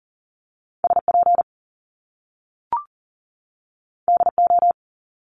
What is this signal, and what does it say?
Morse code, heard as a keyed tone, about 38 wpm, 700 Hz. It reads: SP DO
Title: Top Two Letter Combinations in Words 40wpm